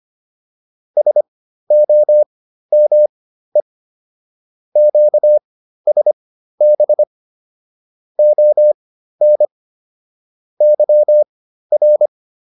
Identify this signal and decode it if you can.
Morse code, heard as a keyed tone, about 25 wpm, spaced as for 13 wpm, 600 Hz. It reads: SOME QSB ON YR
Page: intermediate-practice-files